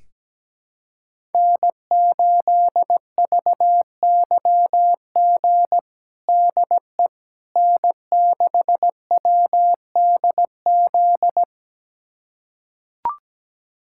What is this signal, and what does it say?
Morse code, heard as a keyed tone, about 17 wpm, 700 Hz. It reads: N8VYG DE N6WDZ
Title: USA Callsign DE Callsign - 5 Characters - 17wpm